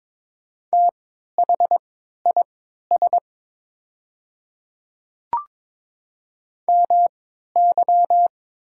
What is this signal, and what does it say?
Morse code, heard as a keyed tone, about 22 wpm, spaced as for 12 wpm, 700 Hz. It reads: THIS MY
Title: 3x Character Spacing - Top 200 Words - 22wpm